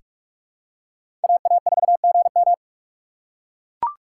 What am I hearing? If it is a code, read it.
Morse code, heard as a keyed tone, about 43 wpm, 700 Hz. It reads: AA4GM
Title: USA Call Signs - 5 Characters - 2x Character Spacing - 45wpm